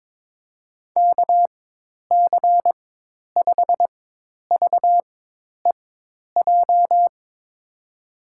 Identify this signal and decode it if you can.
Morse code, heard as a keyed tone, about 22 wpm, spaced as for 10 wpm, 700 Hz. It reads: KC5VEJ
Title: USA Call Signs - 4x Character Spacing - 22wpm